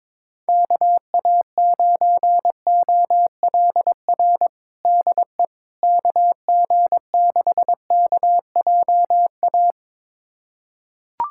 Morse code, heard as a keyed tone, about 22 wpm, 700 Hz. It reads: KA9OLR DE KG6KJA
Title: USA Callsign DE Callsign - 6 Characters - 22wpm